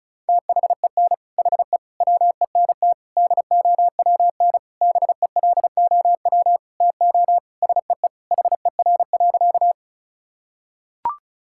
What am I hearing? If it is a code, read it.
Morse code, heard as a keyed tone, about 34 wpm, 700 Hz. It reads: THEN HE WENT DOWN BELOW TO SEE HER.
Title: Sentences from Top 1000 Words - Review Part 5 - 35wpm